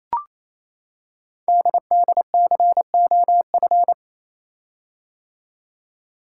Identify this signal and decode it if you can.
Morse code, heard as a keyed tone, about 28 wpm, 700 Hz. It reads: DDCOF
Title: Code Groups - 5 Letters - 28wpm